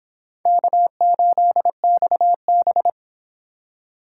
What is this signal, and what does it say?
Morse code, heard as a keyed tone, about 25 wpm, 700 Hz. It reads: K8XB